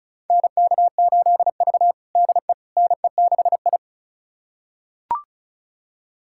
Morse code, heard as a keyed tone, about 36 wpm, 700 Hz. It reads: NK8V DE NE6I
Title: USA Callsign DE Callsign - 4 Characters - 35wpm